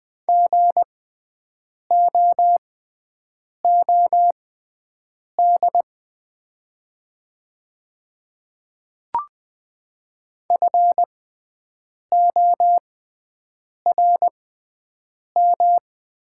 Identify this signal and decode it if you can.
Morse code, heard as a keyed tone, about 20 wpm, spaced as for 7 wpm, 700 Hz. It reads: GOOD FORM
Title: 6x Character Spacing - Top 300 Words - 20wpm